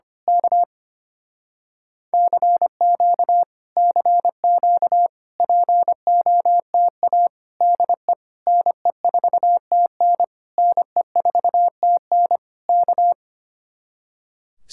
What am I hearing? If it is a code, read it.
Morse code, heard as a keyed tone, about 24 wpm, 700 Hz. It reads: K CQ CQ POTA DE NE4TN NE4TN K
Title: Speed-Racing Common QSO Phrases Encore - 25wpm